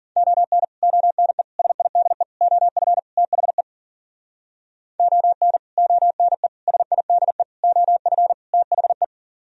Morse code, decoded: ON ONE SIDE OF THE ON ONE SIDE OF THE

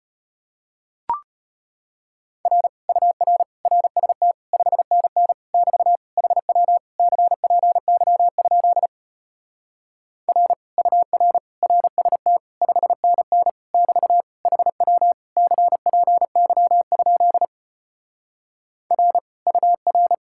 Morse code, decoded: R UR RST 5NN <BT> HW CPY? R UR RST 5NN <BT> HW CPY? R UR

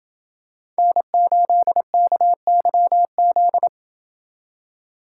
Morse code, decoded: N8KYZ

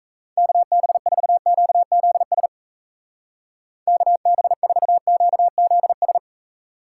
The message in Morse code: KB4QZS KB4QZS

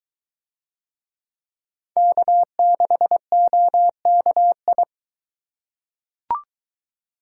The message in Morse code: K6OKI

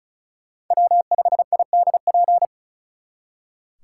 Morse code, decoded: W5IDP

